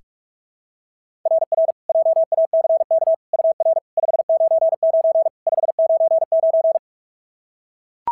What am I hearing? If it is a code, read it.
Morse code, heard as a keyed tone, about 44 wpm, 690 Hz. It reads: RR JACK UR 599 599
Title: SOTA QSO Practice 45wpm